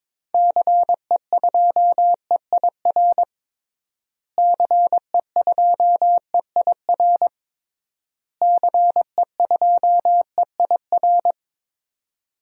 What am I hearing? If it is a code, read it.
Morse code, heard as a keyed tone, about 21 wpm, 700 Hz. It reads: CE2EIR CE2EIR CE2EIR
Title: DXCC-100 Call Signs - 6 Characters - Repeated 3x - 22wpm